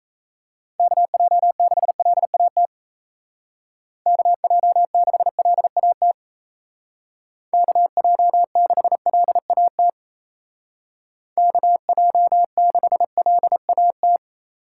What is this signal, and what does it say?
Morse code, heard as a keyed tone, about 34 wpm, 700 Hz. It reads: KJ6LAT KJ6LAT KJ6LAT KJ6LAT
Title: Speed Racing - USA 2x3 Call Signs 28wpm